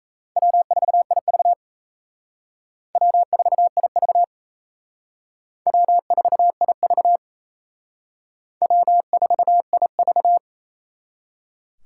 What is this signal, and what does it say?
Morse code, heard as a keyed tone, about 36 wpm, 700 Hz. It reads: W4IV W4IV W4IV W4IV